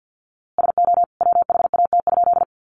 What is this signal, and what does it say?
Morse code, heard as a keyed tone, about 50 wpm, 700 Hz. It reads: SO WHAT?
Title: A Sound Of Thunder 50wpm